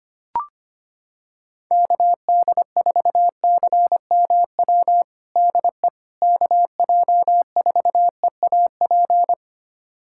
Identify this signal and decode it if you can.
Morse code, heard as a keyed tone, about 25 wpm, 700 Hz. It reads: KD4CMW DE KJ4EAP
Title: USA Callsign DE Callsign 25wpm